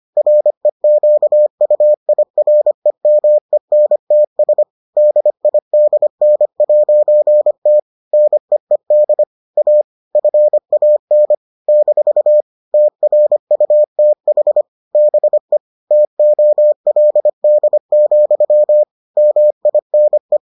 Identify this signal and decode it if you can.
Morse code, heard as a keyed tone, about 25 wpm, 600 Hz. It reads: REQUIREMENTS DIDN'T NEED A FAN - TRUTH BE TOLD, MINE